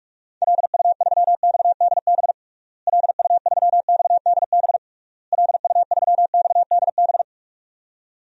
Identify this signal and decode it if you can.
Morse code, heard as a keyed tone, about 41 wpm, 700 Hz. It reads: LU3XDB LU3XDB LU3XDB
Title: DXCC-100 Call Signs - 6 Characters - Repeated 3x - 45wpm